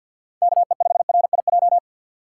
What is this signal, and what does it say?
Morse code, heard as a keyed tone, about 50 wpm, 700 Hz. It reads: KE5RIJ
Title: USA Call Signs - 4x Character Spacing - 50wpm